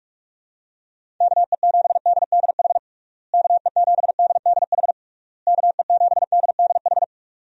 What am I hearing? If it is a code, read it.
Morse code, heard as a keyed tone, about 42 wpm, 700 Hz. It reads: KE7DDH KE7DDH KE7DDH
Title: USA Call Signs - 6 Characters - Repeated 3x - 45wpm